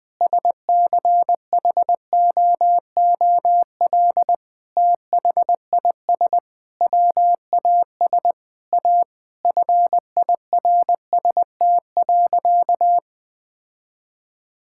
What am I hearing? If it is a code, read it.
Morse code, heard as a keyed tone, about 20 wpm, 700 Hz. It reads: SCHOOL THIS WAS A FIRST.
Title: Sentences with 5-Word Phrases - 20wpm